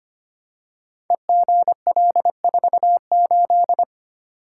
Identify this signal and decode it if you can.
Morse code, heard as a keyed tone, about 25 wpm, 700 Hz. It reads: EGL48
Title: Code Groups - 5 Letter:Numbers - 25wpm